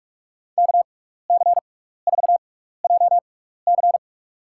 Morse code, heard as a keyed tone, about 43 wpm, spaced as for 15 wpm, 700 Hz. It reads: KC4JC
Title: USA Call Signs - 5 Characters - 6x Character Spacing - 45wpm